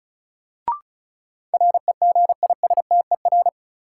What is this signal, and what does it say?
Morse code, heard as a keyed tone, about 36 wpm, 700 Hz. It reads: REGISTER
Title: Sentences from Top 1200 Words - New Word Drill - 35wpm